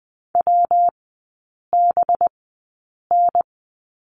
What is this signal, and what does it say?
Morse code, heard as a keyed tone, about 20 wpm, spaced as for 8 wpm, 700 Hz. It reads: WBN
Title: 184 Lesson 30e - Letter b - Three Characters 20wpm